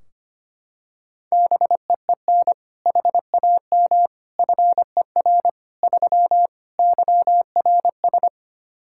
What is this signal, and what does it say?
Morse code, heard as a keyed tone, about 25 wpm, 700 Hz. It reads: BEEN HAM FER 3 YRS
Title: K1USN SST QSO Practice 25wpm